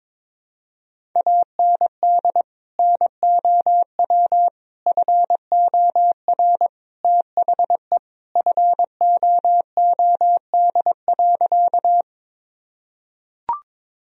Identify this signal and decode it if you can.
Morse code, heard as a keyed tone, about 22 wpm, 700 Hz. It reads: AND NOW FOR THE FOOD.
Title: Sentences from Top 1000 Words - Review Part 1 - 22wpm